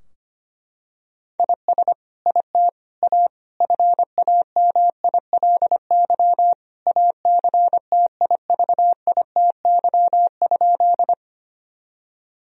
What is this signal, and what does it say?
Morse code, heard as a keyed tone, about 25 wpm, 700 Hz. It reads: IS IT A FAMILY ACTIVITY?
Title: Sentences from Top 1000 Words - Review Part 4 - 25wpm